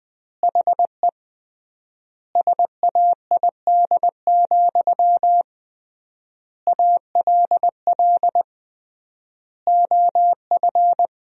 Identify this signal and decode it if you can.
Morse code, heard as a keyed tone, about 20 wpm, 700 Hz. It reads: HE SAID, ALL OF
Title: 3x Word-Spacing - Sentences from Top 500 Words - 20wpm